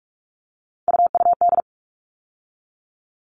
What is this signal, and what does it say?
Morse code, heard as a keyed tone, about 44 wpm, 700 Hz. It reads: UUD
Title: All Cut Numbers - 3-Digit 45wpm